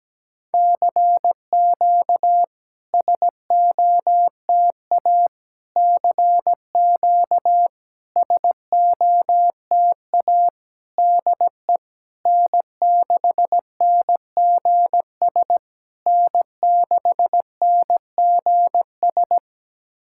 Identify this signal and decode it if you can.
Morse code, heard as a keyed tone, about 19 wpm, 700 Hz. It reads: CQ SOTA CQ SOTA DE N6NGS N6NGS
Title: SOTA QSO Practice 17wpm